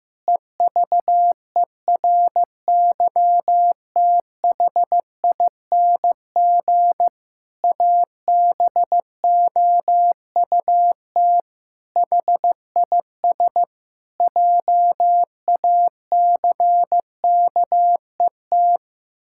Morse code, heard as a keyed tone, about 15 wpm, 700 Hz. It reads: EVERYTHING ABOUT HIS JACKET
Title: Pangram Challenge v2 - 15wpm